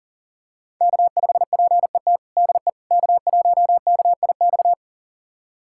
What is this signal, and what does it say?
Morse code, heard as a keyed tone, about 37 wpm, 700 Hz. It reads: K5PET DE K1KIX
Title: USA Callsign DE Callsign - 5 Characters - 40wpm